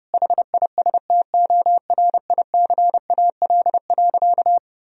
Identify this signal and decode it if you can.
Morse code, heard as a keyed tone, about 30 wpm, 700 Hz. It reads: HISTORICAL.
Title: Sentences from Top 1000 Words - Review Part 4 - 30wpm